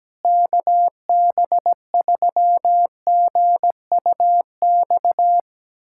Morse code, heard as a keyed tone, about 17 wpm, 700 Hz. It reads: KB3GUX
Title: USA Callsign DE Callsign - 6 Characters - 17wpm